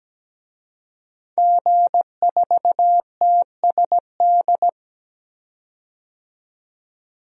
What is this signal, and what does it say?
Morse code, heard as a keyed tone, about 17 wpm, 700 Hz. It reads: G4TSD
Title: DXCC-100 Call Signs - 5 Characters - 17wpm